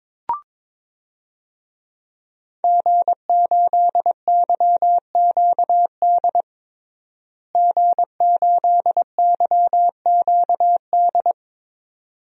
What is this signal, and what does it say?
Morse code, heard as a keyed tone, about 21 wpm, 700 Hz. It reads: G8YQD G8YQD
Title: DXCC-100 Call Signs - 5 Characters - Repeated 2x - 22wpm